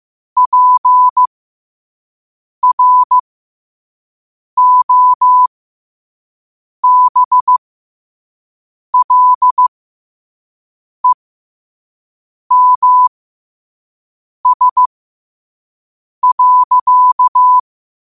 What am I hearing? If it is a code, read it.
Morse code, heard as a keyed tone, about 15 wpm, spaced as for 5 wpm, 1000 Hz. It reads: PROBLEMS.
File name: 5wpm